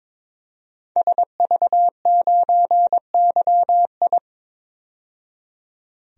Morse code, heard as a keyed tone, about 21 wpm, 700 Hz. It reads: SV9YI
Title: DXCC-100 Call Signs - 5 Characters - 22wpm